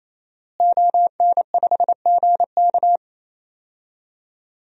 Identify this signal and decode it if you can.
Morse code, heard as a keyed tone, about 28 wpm, 700 Hz. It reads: ON5GK
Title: DXCC-100 Call Signs - 5 Characters - Repeated 3x - 28wpm